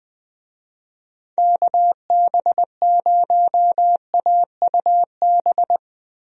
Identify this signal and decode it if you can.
Morse code, heard as a keyed tone, about 20 wpm, 700 Hz. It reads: KB0AUB
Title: USA Call Signs - 6 Characters - Repeated 3x - 20wpm